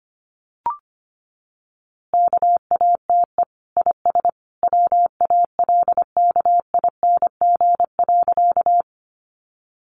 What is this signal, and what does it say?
Morse code, heard as a keyed tone, about 25 wpm, 700 Hz. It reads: KATE IS WALKING.
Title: Easy Phrases 25wpm